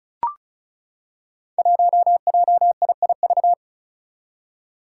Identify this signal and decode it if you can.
Morse code, heard as a keyed tone, about 36 wpm, 700 Hz. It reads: 1JIIV
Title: Five Challenging Letter-Number Combinations 35wpm